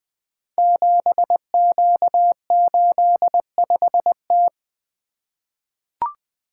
Morse code, heard as a keyed tone, about 20 wpm, 700 Hz. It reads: 7Q85T
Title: Code Groups - 5 Letter:Numbers - 20wpm